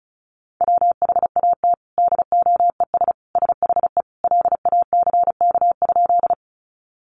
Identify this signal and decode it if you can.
Morse code, heard as a keyed tone, about 34 wpm, 700 Hz. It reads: WHAT DOES SHE LACK?
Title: Sentences from Top 1100 Words 35wpm